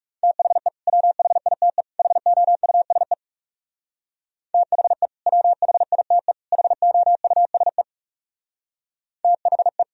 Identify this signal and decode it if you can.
Morse code, heard as a keyed tone, about 36 wpm, 700 Hz. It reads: THE WHITE HOUSE THE WHITE HOUSE THE
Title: Speed-Racing - Sets of 3 Words 30wpm